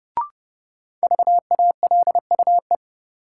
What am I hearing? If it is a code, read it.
Morse code, heard as a keyed tone, about 30 wpm, 700 Hz. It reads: VALUE
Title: Sentences from Top 600 Words - New Word Drill - 30wpm